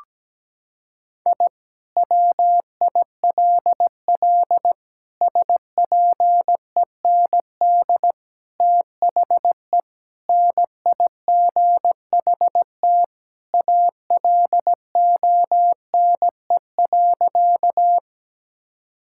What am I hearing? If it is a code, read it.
Morse code, heard as a keyed tone, about 17 wpm, 700 Hz. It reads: I WILL SPEND THE NIGHT ALONE.
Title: Sentences from Top 1000 Words - Review Part 5 - 17wpm